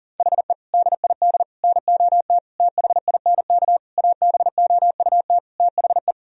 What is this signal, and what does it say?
Morse code, heard as a keyed tone, about 37 wpm, 700 Hz. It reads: HE DID NOT THINK ABOUT THE
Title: Sentences from Top 1000 Words - Review Part 3 - 40wpm